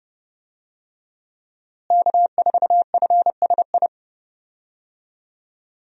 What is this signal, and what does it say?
Morse code, heard as a keyed tone, about 32 wpm, 700 Hz. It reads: K4FSI
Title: USA Call Signs - 5 Characters - Repeated 2x - 30wpm